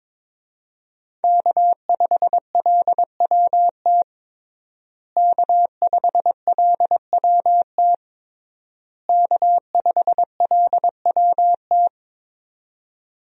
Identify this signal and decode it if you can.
Morse code, heard as a keyed tone, about 22 wpm, 700 Hz. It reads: K5LWT K5LWT K5LWT
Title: USA Call Signs - 5 Characters - Repeated 3x - 22wpm